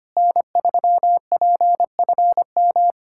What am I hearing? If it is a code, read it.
Morse code, heard as a keyed tone, about 25 wpm, 700 Hz. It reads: N3PFM
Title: K1USN SST QSO Practice - 25wpm